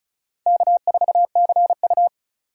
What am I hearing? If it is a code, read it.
Morse code, heard as a keyed tone, about 36 wpm, 700 Hz. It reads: K4CU
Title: USA Call Signs - 4 Characters - 4x Character Spacing - 35wpm